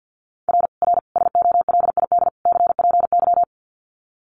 Morse code, heard as a keyed tone, about 50 wpm, 700 Hz. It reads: R R SOLID CPY
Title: Common QSO Phrases 50wpm